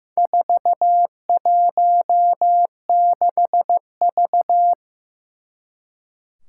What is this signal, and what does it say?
Morse code, heard as a keyed tone, about 15 wpm, 700 Hz. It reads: 416V
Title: Four Challenging Letter-Number Combinations 15wpm